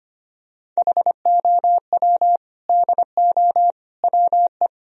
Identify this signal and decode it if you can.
Morse code, heard as a keyed tone, about 25 wpm, 700 Hz. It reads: HOW DO WE
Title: Sentences from Top 1000 Words - Review Part 3 - 25wpm